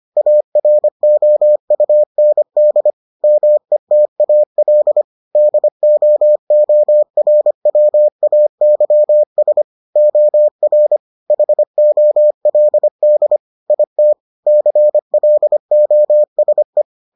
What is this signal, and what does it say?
Morse code, heard as a keyed tone, about 25 wpm, 600 Hz. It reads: AROUND METAL DOORWAYS OR HOLD IT CLOSE